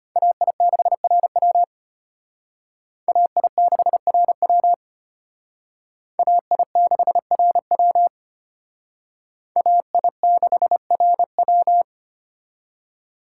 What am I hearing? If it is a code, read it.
Morse code, heard as a keyed tone, about 32 wpm, 700 Hz. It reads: AI6RW AI6RW AI6RW AI6RW